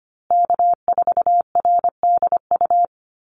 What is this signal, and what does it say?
Morse code, heard as a keyed tone, about 25 wpm, 700 Hz. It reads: K4RDU
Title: USA Call Signs Repeated 3x 25wpm